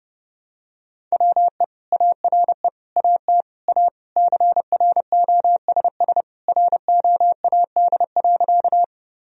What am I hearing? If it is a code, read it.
Morse code, heard as a keyed tone, about 30 wpm, 700 Hz. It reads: WE ARE AT A CROSS ROAD.
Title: Sentences from Top 1000 Words - Review Part 2 - 30wpm